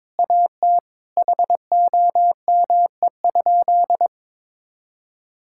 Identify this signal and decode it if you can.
Morse code, heard as a keyed tone, about 22 wpm, 700 Hz. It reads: AT HOME?
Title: Sentences from Top 1000 Words - Review Part 5 - 22wpm